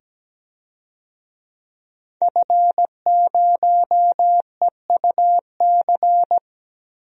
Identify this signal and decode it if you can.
Morse code, heard as a keyed tone, about 17 wpm, 700 Hz. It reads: F0EUC
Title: DXCC-100 Call Signs - 5 Characters - 17wpm